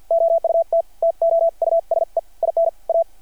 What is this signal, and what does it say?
Morse code, heard as a keyed tone, about 52 wpm, spaced as for 44 wpm, 660 Hz. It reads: OUT TO USE IN A
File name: archived / CRSnet